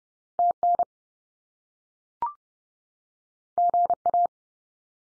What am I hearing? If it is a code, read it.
Morse code, heard as a keyed tone, about 30 wpm, 700 Hz. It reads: TN GA
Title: US States 30WPM